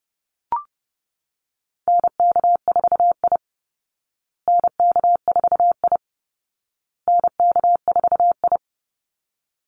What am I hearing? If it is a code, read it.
Morse code, heard as a keyed tone, about 30 wpm, 700 Hz. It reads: NK4I NK4I NK4I
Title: Call Signs Repeated 2x - 30wpm